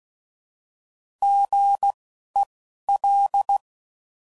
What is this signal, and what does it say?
Morse code, heard as a keyed tone, about 16 wpm, spaced as for 11 wpm, 800 Hz. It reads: GEL